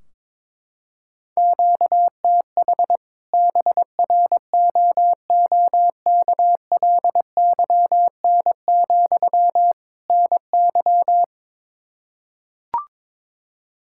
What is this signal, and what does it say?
Morse code, heard as a keyed tone, about 21 wpm, 700 Hz. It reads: QTH BROOKLYN, NY
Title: Practice QSO - Exchange 1 - Beginning - 22wpm